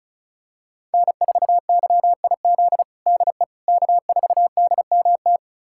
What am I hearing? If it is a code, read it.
Morse code, heard as a keyed tone, about 36 wpm, 700 Hz. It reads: N4YIZ DE K4DMT
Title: USA Callsign DE Callsign - 5 Characters - 35wpm